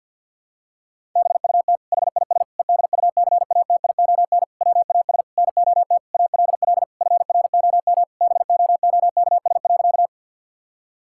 Morse code, heard as a keyed tone, about 51 wpm, 700 Hz. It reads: BUT HIS EDUCATION WAS NOT ALL FROM BOOKS.
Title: Sentences from Top 1000 Words - Review Part 6 - 50wpm